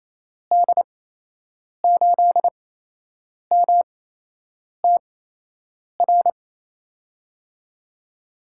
Morse code, heard as a keyed tone, about 28 wpm, spaced as for 8 wpm, 700 Hz. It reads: D8MTR
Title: USA Call Signs - 6 Characters - 8x Character Spacing - 28wpm